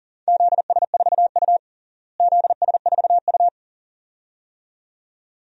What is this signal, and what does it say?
Morse code, heard as a keyed tone, about 40 wpm, 700 Hz. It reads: ZS4U ZS4U
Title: DXCC-100 Call Signs - 4 Characters - Repeated 2x - 40wpm